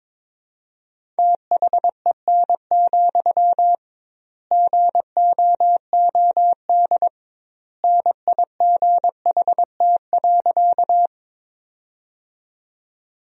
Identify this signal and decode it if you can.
Morse code, heard as a keyed tone, about 22 wpm, 700 Hz. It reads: THEN, GOOD NIGHT.
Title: 2x Word-Spacing - Sentences from Top 300 Words - 22wpm